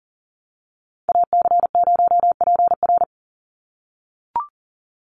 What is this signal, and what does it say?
Morse code, heard as a keyed tone, about 40 wpm, 700 Hz. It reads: AC0PR
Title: NA SOTA Activator Callsigns 40wpm